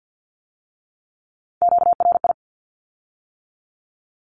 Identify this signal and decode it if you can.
Morse code, heard as a keyed tone, about 50 wpm, 700 Hz. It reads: QRI
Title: Q-Codes with Spoken Meaning - 50wpm